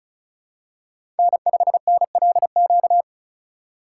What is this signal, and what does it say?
Morse code, heard as a keyed tone, about 36 wpm, 700 Hz. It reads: N5NLQ
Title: USA Call Signs - 5 Characters - 4x Character Spacing - 35wpm